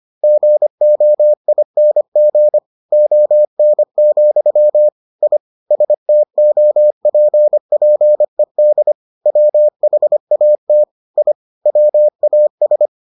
Morse code, heard as a keyed tone, about 25 wpm, 600 Hz. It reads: GOING ON, I STOPPED WHAT I WAS